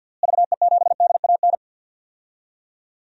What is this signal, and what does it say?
Morse code, heard as a keyed tone, about 51 wpm, 700 Hz. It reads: VE7DAN
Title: DXCC-100 Call Signs - 6 Characters - Repeated 2x - 50wpm